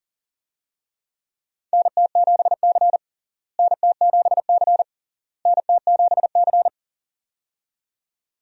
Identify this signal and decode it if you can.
Morse code, heard as a keyed tone, about 40 wpm, 700 Hz. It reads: NT7C NT7C NT7C
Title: USA Call Signs - 4 Characters - Repeated 3x - 40wpm